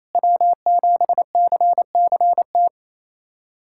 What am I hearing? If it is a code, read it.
Morse code, heard as a keyed tone, about 27 wpm, 700 Hz. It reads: W7CCT
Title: USA Callsign DE Callsign - 5 Characters - 28wpm